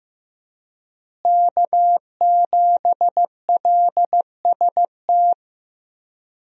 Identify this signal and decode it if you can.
Morse code, heard as a keyed tone, about 15 wpm, 700 Hz. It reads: K7LST
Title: USA Call Signs - 5 Characters - Repeated 2x - 15wpm